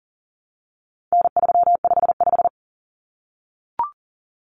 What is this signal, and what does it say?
Morse code, heard as a keyed tone, about 40 wpm, 700 Hz. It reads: N355
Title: Common Cut Numbers - 4-Digit 40wpm